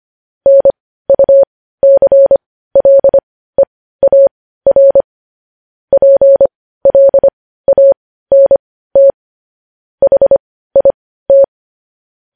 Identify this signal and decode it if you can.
Morse code, heard as a keyed tone, about 25 wpm, spaced as for 15 wpm, 550 Hz. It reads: NUCLEAR PLANT HIT